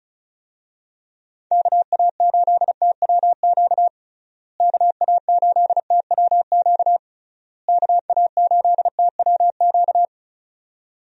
Morse code, heard as a keyed tone, about 36 wpm, 700 Hz. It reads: KA8TWQ KA8TWQ KA8TWQ
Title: USA Call Signs - 6 Characters - Repeated 3x - 35wpm